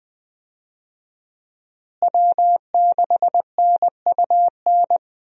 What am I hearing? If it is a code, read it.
Morse code, heard as a keyed tone, about 20 wpm, 700 Hz. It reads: W6NUN
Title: USA Call Signs - 5 Characters - Repeated 2x - 20wpm